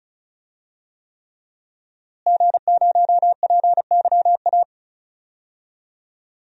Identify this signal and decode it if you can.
Morse code, heard as a keyed tone, about 36 wpm, 700 Hz. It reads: G0PYA
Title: DXCC-100 Call Signs - 5 Characters - Repeated 3x - 35wpm